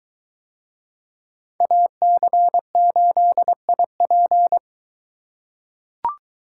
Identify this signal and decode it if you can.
Morse code, heard as a keyed tone, about 22 wpm, 700 Hz. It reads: AC8IP